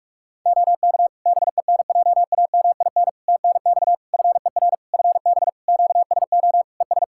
Morse code, heard as a keyed tone, about 46 wpm, 700 Hz. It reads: OK BENJAMIN TNX FER FB QSO ES
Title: Practice QSO - Exchange 4 - Ending - 45wpm